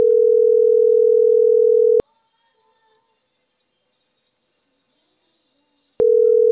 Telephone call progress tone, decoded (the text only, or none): ringback tone